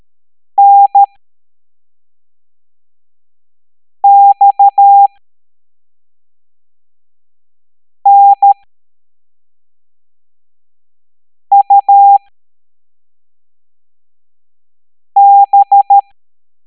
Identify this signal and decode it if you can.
Morse code, heard as a keyed tone, about 13 wpm, spaced as for 3 wpm, 800 Hz. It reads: NXNUB